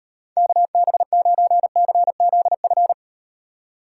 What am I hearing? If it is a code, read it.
Morse code, heard as a keyed tone, about 38 wpm, 700 Hz. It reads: KB9CZF